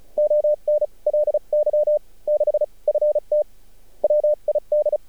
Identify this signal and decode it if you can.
Morse code, heard as a keyed tone, about 34 wpm, spaced as for 30 wpm, 600 Hz. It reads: ONLY 6 FT WID
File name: archived / CRSnet